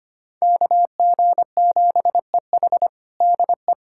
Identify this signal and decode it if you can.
Morse code, heard as a keyed tone, about 25 wpm, 700 Hz. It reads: KG7EH DE
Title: USA Callsign DE Callsign - 5 Characters - 25wpm